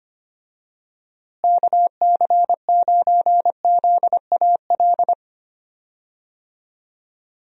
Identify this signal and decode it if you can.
Morse code, heard as a keyed tone, about 25 wpm, 700 Hz. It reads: KC9ZAL
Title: USA Call Signs - 6 Characters - Repeated 2x - 25wpm